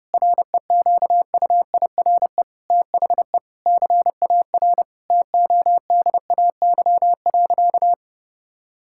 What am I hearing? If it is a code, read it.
Morse code, heard as a keyed tone, about 30 wpm, 700 Hz. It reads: REQUIRE THE CAR TODAY.
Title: Sentences from Top 1000 Words - Review Part 6 - 30wpm